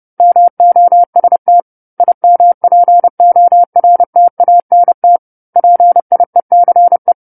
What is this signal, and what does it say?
Morse code, heard as a keyed tone, about 30 wpm, 700 Hz. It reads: MOST IMPORTANT PIECE
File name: 10-Chapter-10 30 WPM